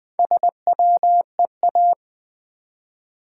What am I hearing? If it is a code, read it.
Morse code, heard as a keyed tone, about 20 wpm, 700 Hz. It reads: SWEA
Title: Build A Word - 1 Letter At A Time - 3 to 5-Letter Words - 20wpm